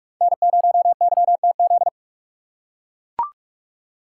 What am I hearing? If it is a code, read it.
Morse code, heard as a keyed tone, about 42 wpm, 700 Hz. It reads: N0YTZ